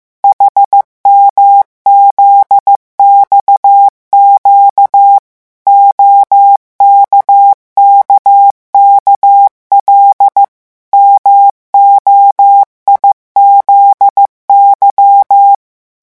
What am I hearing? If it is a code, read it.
Morse code, heard as a keyed tone, about 15 wpm, 800 Hz. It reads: HMZXQ OKKKL MOIZY
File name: morse mp3 / revision11 lettres vitesse 14